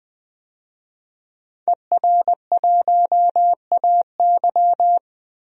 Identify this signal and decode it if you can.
Morse code, heard as a keyed tone, about 21 wpm, 700 Hz. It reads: ER1AY
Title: DXCC-100 Call Signs - 5 Characters - Repeated 2x - 20wpm